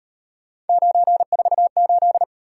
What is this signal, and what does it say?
Morse code, heard as a keyed tone, about 38 wpm, 700 Hz. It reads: 948